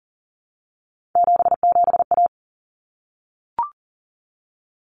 Morse code, heard as a keyed tone, about 40 wpm, 700 Hz. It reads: 77A
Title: Common Cut Numbers - 3-Digit 40wpm